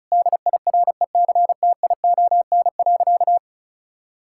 Morse code, heard as a keyed tone, about 36 wpm, 700 Hz. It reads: DIRECTION.
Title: Sentences from Top 1000 Words - Review Part 5 - 35wpm